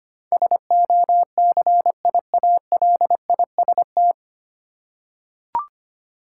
Morse code, heard as a keyed tone, about 25 wpm, 700 Hz. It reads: SOCIALIST
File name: practice / 25wpm / 9 Letter Words - First 3 Letters Given - 25wpm